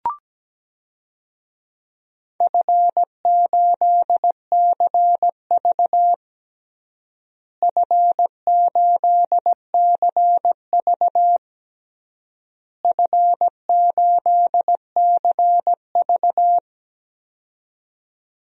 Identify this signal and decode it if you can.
Morse code, heard as a keyed tone, about 17 wpm, 700 Hz. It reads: F8CV F8CV F8CV
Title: DXCC-100 Call Signs - 4 Characters - Repeated 3x - 17wpm